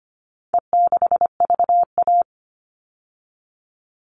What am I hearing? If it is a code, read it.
Morse code, heard as a keyed tone, about 25 wpm, 700 Hz. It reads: E6VA
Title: All Cut Numbers - 4-Digit 25wpm